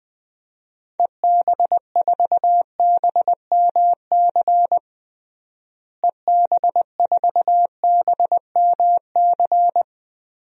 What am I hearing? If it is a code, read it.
Morse code, heard as a keyed tone, about 20 wpm, 700 Hz. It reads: EB4BMC EB4BMC
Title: DXCC-100 Call Signs - 6 Characters - Repeated 3x - 20wpm